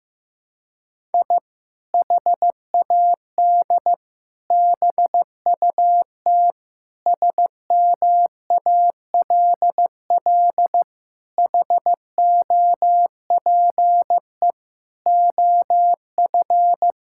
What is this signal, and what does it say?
Morse code, heard as a keyed tone, about 15 wpm, 700 Hz. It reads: I HAD BUT SMALL HOPE OF
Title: Sentences from Top 1000 Words - Review Part 6 - 15wpm